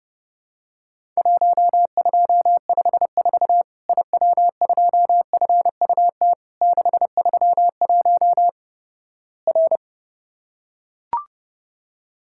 Morse code, heard as a keyed tone, about 30 wpm, 700 Hz. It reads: 1254 IW2FUT 631 R
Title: Worked All Europe DX Contest - QTC Exchange - 30wpm